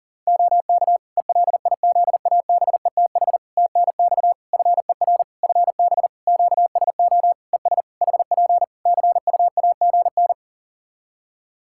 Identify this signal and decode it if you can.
Morse code, heard as a keyed tone, about 40 wpm, 700 Hz. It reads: OK ELIZABETH TNX FER FB QSO ES HP CUAGN
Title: Practice QSO - Exchange 4 - Ending - 40wpm